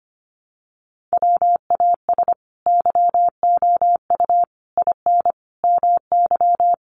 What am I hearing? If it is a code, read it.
Morse code, heard as a keyed tone, about 25 wpm, 700 Hz. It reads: WAS YOU IN MY
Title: Sentences from Top 300 Words 25wpm